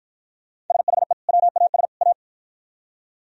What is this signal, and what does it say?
Morse code, heard as a keyed tone, about 52 wpm, 700 Hz. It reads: SHE WAS A